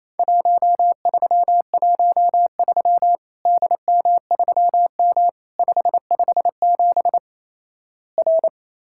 Morse code, heard as a keyed tone, about 28 wpm, 700 Hz. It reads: 1313 DM3M 557 R
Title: Worked All Europe DX Contest - QTC Exchange - 28wpm